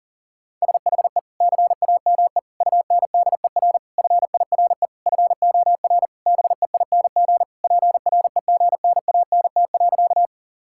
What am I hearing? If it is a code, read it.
Morse code, heard as a keyed tone, about 40 wpm, 700 Hz. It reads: SHE CAME UNDER FIRE FOR BEING PREGNANT.
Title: Sentences from Top 1000 Words - Review Part 6 - 40wpm